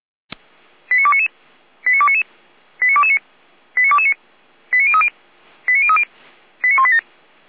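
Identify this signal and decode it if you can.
Five-tone selective call (ZVEI), heard as quick runs of five tones, tones of about 70 ms. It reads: 89299 89299 89229 89229 89030 89030 89187
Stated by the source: ZVEI five-tone selective call 89075 received by SDR.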